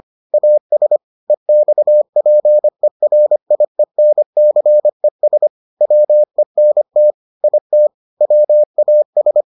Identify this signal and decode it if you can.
Morse code, heard as a keyed tone, about 25 wpm, 600 Hz. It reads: AS EXPERIENCES WENT IT WAS